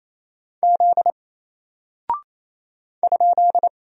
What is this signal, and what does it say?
Morse code, heard as a keyed tone, about 28 wpm, 700 Hz. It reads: Z ?
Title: Single Character (Letter:Number:Punctuation:Prosign) Alt 28wpm